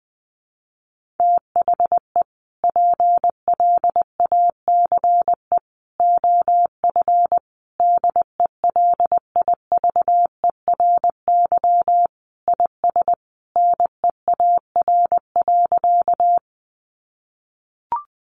Morse code, heard as a keyed tone, about 20 wpm, 700 Hz. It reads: THE PLACE OF DELIVERY IS NEAR.
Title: Sentences from Top 1500 Words 20wpm